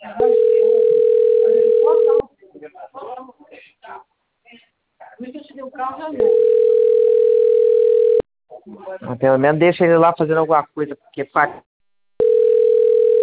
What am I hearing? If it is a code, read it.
Ringback tone (North America), 440 + 480 Hz, with speech.
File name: rg-1600-unknown- - - .90463